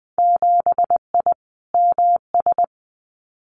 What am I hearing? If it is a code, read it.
Morse code, heard as a keyed tone, about 20 wpm, 700 Hz. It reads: 7I MS
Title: Winter Field Day Exchanges Encore 20wpm